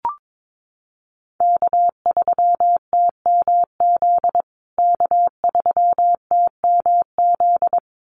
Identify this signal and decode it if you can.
Morse code, heard as a keyed tone, about 22 wpm, 700 Hz. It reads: K3TMZ K3TMZ
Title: USA Call Signs Repeated 2x 22wpm